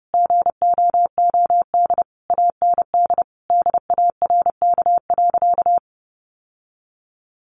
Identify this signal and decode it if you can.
Morse code, heard as a keyed tone, about 30 wpm, 700 Hz. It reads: GOOD AND DARK.
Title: Sentences from Top 600 Words 30wpm